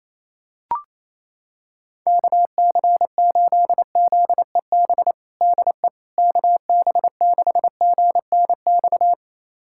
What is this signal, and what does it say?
Morse code, heard as a keyed tone, about 27 wpm, 700 Hz. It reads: KC8ZEB DE KB6GNX
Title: USA Callsign DE Callsign 28wpm